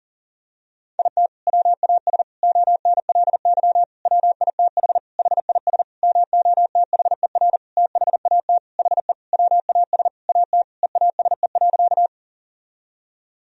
Morse code, heard as a keyed tone, about 40 wpm, 700 Hz. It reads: IT WAS ONLY WITH HIS MOTHER THAT HE WAS AT EASE.
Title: Sentences from Top 1000 Words - Review Part 1 - 40wpm